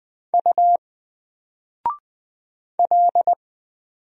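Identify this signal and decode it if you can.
Morse code, heard as a keyed tone, about 20 wpm, 700 Hz. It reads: U L
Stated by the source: Single letter and number Morse drill at 20 wpm, from N4LCD.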